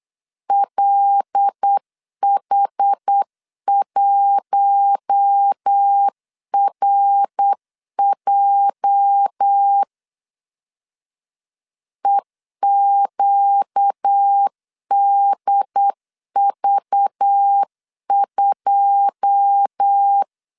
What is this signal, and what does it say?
Morse code, heard as a keyed tone, about 8 wpm, 800 Hz. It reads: LH1RJ EQDV2
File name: LEZIONI / 9 28 8wpm 3512